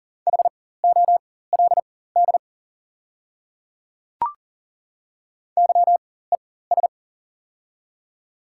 Morse code, heard as a keyed tone, about 40 wpm, spaced as for 19 wpm, 690 Hz. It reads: HOLD YES
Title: 4x Character Spacing - Top 500 Words - 40wpm